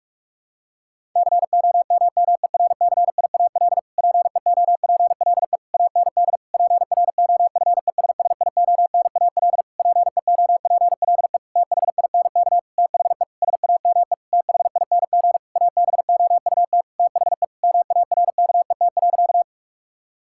Morse code, heard as a keyed tone, about 46 wpm, 700 Hz. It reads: COMMERCIAL PEOPLE AND PROFESSIONAL PEOPLE THINK THE SAME THING ABOUT THE MARKET.
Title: Sentences from Top 1000 Words - Review Part 5 - 45wpm